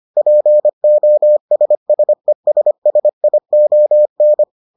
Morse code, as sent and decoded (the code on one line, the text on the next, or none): .--. --- ... ... ... ... .. --- -.
POSSESSION